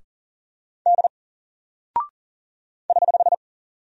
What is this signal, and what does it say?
Morse code, heard as a keyed tone, about 40 wpm, 700 Hz. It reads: D <HH>
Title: Single Character (Letter:Number:Punctuation:Prosign) Alt 40wpm